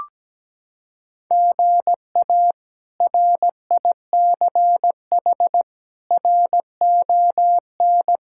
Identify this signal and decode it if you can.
Morse code, heard as a keyed tone, about 17 wpm, 700 Hz. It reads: GA RICH RON
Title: K1USN SST QSO Practice - 17wpm